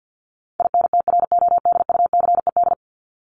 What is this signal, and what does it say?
Morse code, heard as a keyed tone, about 50 wpm, 700 Hz. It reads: INTRODUCED
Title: ED Suffix - Top 100 Words with Base Words 50wpm